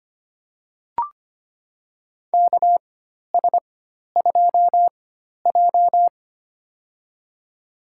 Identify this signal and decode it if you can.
Morse code, heard as a keyed tone, about 25 wpm, spaced as for 12 wpm, 700 Hz. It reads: KS2J
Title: USA Call Signs - 4 Characters - 4x Character Spacing - 25wpm